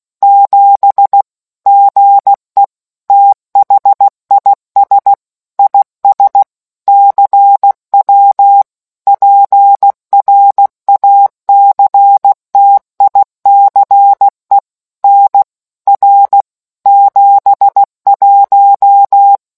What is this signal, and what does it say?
Morse code, heard as a keyed tone, about 16 wpm, 800 Hz. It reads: 7 GE THIS IS CW PRACTICE NR 71